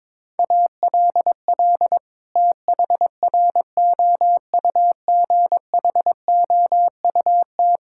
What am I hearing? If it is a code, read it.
Morse code, heard as a keyed tone, about 22 wpm, 700 Hz. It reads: ALL THROUGHOUT
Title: Sentences from Top 1000 Words - Review Part 5 - 22wpm